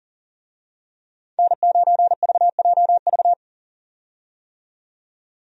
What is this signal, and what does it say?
Morse code, heard as a keyed tone, about 40 wpm, 700 Hz. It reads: N9VJV
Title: USA Call Signs - 5 Characters - 40wpm